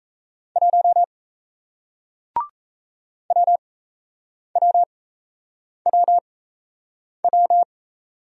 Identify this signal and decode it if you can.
Morse code, heard as a keyed tone, about 36 wpm, spaced as for 8 wpm, 700 Hz. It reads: 1 WWWW